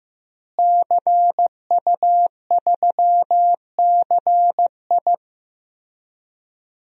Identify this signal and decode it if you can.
Morse code, heard as a keyed tone, about 15 wpm, 700 Hz. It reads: CU3CI
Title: DXCC-100 Call Signs - 5 Characters - Repeated 2x - 15wpm